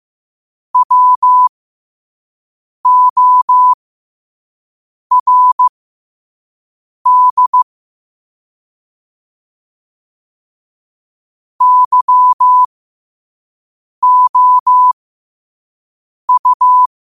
Morse code, heard as a keyed tone, about 15 wpm, spaced as for 5 wpm, 1000 Hz. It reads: WORD YOU
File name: qotdfiles / 5wpm